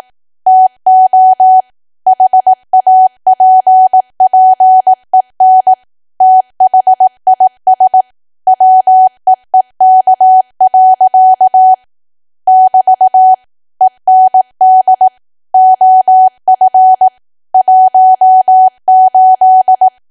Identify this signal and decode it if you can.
Morse code, heard as a keyed tone, about 18 wpm, 750 Hz. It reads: TO HAPPEN THIS WEEK. <BT> END OF 18
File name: cwa / W1AW-Level2 / 18WPM